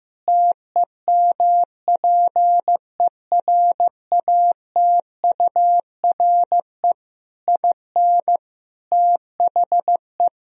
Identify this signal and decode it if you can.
Morse code, heard as a keyed tone, about 16 wpm, 700 Hz. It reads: TEMPERATURE IN THE
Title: Sentences from Top 1000 Words - Review Part 4 - 15wpm